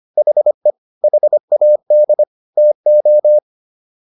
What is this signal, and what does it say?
Morse code, heard as a keyed tone, about 25 wpm, 600 Hz. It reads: HE HAD TO